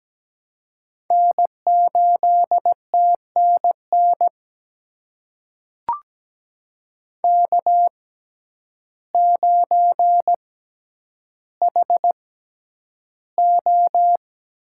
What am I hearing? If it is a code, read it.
Morse code, heard as a keyed tone, about 17 wpm, spaced as for 9 wpm, 700 Hz. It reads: N8TNN K9HO
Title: USA Call Signs - 5 Characters - 6x Character Spacing - 17wpm